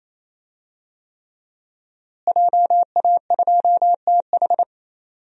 Morse code, heard as a keyed tone, about 28 wpm, 700 Hz. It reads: JA2TH
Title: DXCC-100 Call Signs - 5 Characters - Repeated 3x - 28wpm